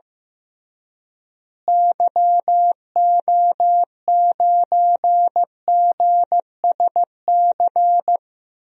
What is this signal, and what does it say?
Morse code, heard as a keyed tone, about 15 wpm, 700 Hz. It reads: YO9GSC
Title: DXCC-100 Call Signs - 6 Characters - Repeated 3x - 15wpm